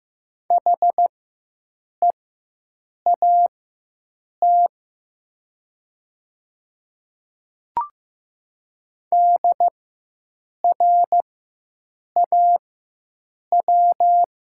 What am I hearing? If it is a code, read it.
Morse code, heard as a keyed tone, about 15 wpm, spaced as for 7 wpm, 700 Hz. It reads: HEAT DRAW
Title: 4x Character Spacing - Top 500 Words - 15wpm